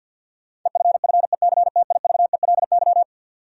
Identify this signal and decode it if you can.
Morse code, heard as a keyed tone, about 47 wpm, 700 Hz. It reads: EFFECTIVELY
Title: Sentences from Top 1400 Words - New Word Drill - 50wpm